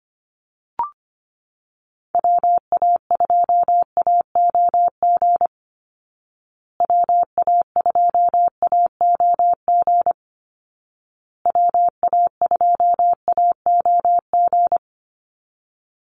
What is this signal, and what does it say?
Morse code, heard as a keyed tone, about 25 wpm, 700 Hz. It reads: WA2AOG WA2AOG WA2AOG
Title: Call Signs Repeated 2x - 25wpm